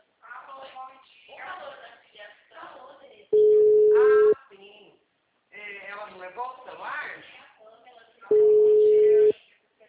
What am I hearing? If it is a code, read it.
Ringback tone (Europe, 425 Hz), with speech.